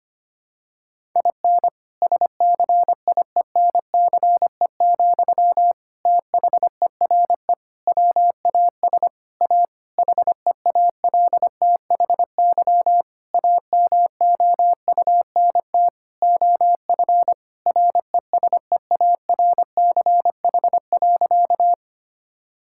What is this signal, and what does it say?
Morse code, heard as a keyed tone, about 25 wpm, 700 Hz. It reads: IN SCIENCE, THERE WAS A HEALTHY AMOUNT OF RESEARCH.
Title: Sentences from Top 1000 Words - Review Part 5 - 25wpm